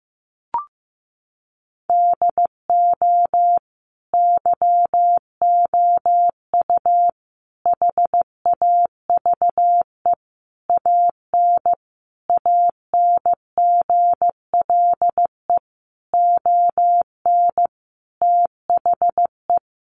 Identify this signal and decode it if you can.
Morse code, heard as a keyed tone, about 15 wpm, 700 Hz. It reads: DO YOU HAVE AN ANGLE ON THE
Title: Sentences from Top 1500 Words 15wpm